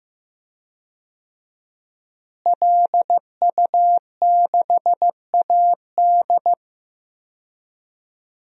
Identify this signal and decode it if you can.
Morse code, heard as a keyed tone, about 15 wpm, 700 Hz. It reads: LU6AD
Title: DXCC-100 Call Signs - 5 Characters - Repeated 3x - 15wpm